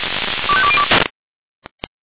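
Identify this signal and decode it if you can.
Five-tone selective call (ZVEI), heard as quick runs of five tones, tones of about 70 ms. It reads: 25202
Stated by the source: ZVEI five-tone call 25345 received by SDR.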